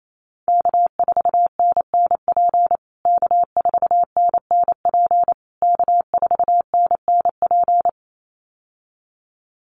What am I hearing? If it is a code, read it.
Morse code, heard as a keyed tone, about 28 wpm, 690 Hz. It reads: K4NNP K4NNP K4NNP
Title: USA Call Signs Repeated 3x 28wpm